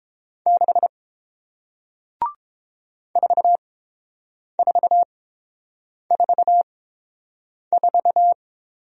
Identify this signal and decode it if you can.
Morse code, heard as a keyed tone, about 30 wpm, spaced as for 7 wpm, 700 Hz. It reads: 6 4444